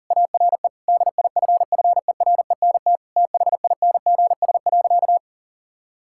Morse code, heard as a keyed tone, about 40 wpm, 700 Hz. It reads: ARE DIFFERENT THINGS.
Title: Sentences from Top 1000 Words - Review Part 6 - 40wpm